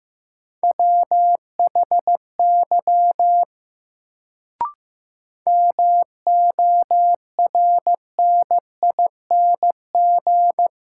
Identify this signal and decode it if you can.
Morse code, heard as a keyed tone, about 15 wpm, 700 Hz. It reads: WHY MORNING
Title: Sets of 3 Words - New Word Drill - 15wpm